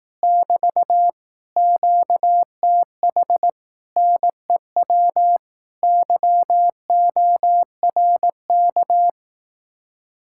Morse code, heard as a keyed tone, about 18 wpm, 700 Hz. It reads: <BT> QTH NEW YORK
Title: Speed-Racing Common QSO Phrases Encore - 15wpm